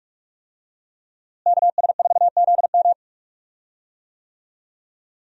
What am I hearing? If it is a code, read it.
Morse code, heard as a keyed tone, about 46 wpm, 700 Hz. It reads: KS4ZM
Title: USA Call Signs - 5 Characters - Repeated 2x - 45wpm